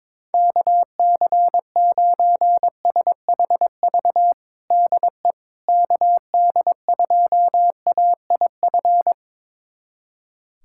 Morse code, heard as a keyed tone, about 22 wpm, 700 Hz. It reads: KC9SHV DE KD2AIF
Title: USA Callsign DE Callsign - 6 Characters - 22wpm